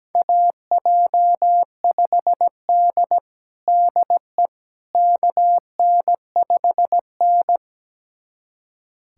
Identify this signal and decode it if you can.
Morse code, heard as a keyed tone, about 17 wpm, 700 Hz. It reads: AJ5D DE KN5N
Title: USA Callsign DE Callsign - 4 Characters - 17wpm